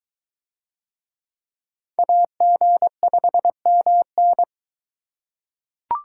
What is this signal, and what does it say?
Morse code, heard as a keyed tone, about 23 wpm, 700 Hz. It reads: AG5MN